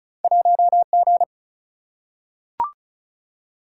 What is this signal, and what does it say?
Morse code, heard as a keyed tone, about 36 wpm, 700 Hz. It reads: 1 G
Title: Single-Letter Single-Number 35wpm